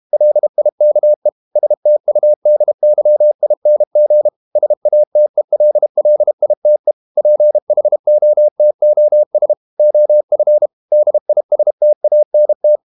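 Morse code, heard as a keyed tone, about 32 wpm, 600 Hz. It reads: LIKE STUDYING SATELLITE PHOTOS OF DISTANT
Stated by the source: Morse code practice at 32 wpm from CW med Gnister, reading Ken Follett's novel Never in English.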